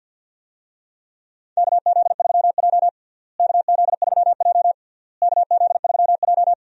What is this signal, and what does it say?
Morse code, heard as a keyed tone, about 47 wpm, 700 Hz. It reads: KZ3J KZ3J KZ3J
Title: USA Call Signs - 4 Characters - Repeated 3x - 50wpm